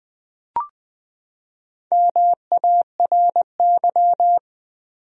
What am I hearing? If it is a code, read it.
Morse code, heard as a keyed tone, about 20 wpm, 700 Hz. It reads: MARY
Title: USA Common Names - Max 4 Letters - 20wpm